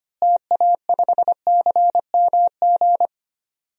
Morse code, heard as a keyed tone, about 25 wpm, 700 Hz. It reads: TA5CMG
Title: DXCC-100 Call Signs - 6 Characters - Repeated 2x - 25wpm